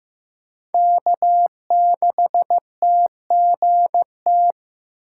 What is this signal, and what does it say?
Morse code, heard as a keyed tone, about 15 wpm, 700 Hz. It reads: K6TGT
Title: USA Call Signs - 2x Character Spacing - 15wpm